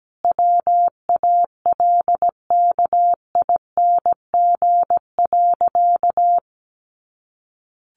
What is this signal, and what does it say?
Morse code, heard as a keyed tone, about 17 wpm, 700 Hz. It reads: WALKING.
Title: Easy Phrases 17wpm